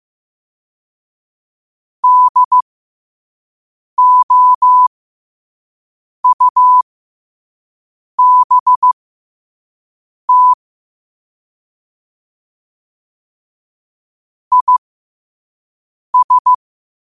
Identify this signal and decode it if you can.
Morse code, heard as a keyed tone, about 15 wpm, spaced as for 5 wpm, 1000 Hz. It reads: DOUBT IS
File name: qotdfiles / 5wpm